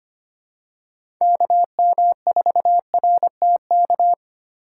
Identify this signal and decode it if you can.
Morse code, heard as a keyed tone, about 25 wpm, 700 Hz. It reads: KM4RTK
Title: USA Call Signs - 6 Characters - 4x Character Spacing - 25wpm